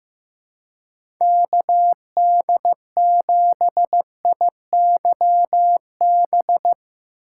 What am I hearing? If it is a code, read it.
Morse code, heard as a keyed tone, about 15 wpm, 700 Hz. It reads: KD7IYB
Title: USA Call Signs - 6 Characters - 3x Character Spacing - 15wpm